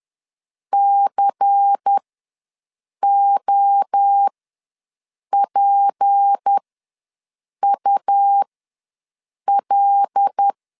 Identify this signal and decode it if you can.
Morse code, heard as a keyed tone, about 10 wpm, spaced as for 6 wpm, 800 Hz. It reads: COPUL
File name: LEZIONI / 6 PKX0 / 6 P ws 10wpm 3751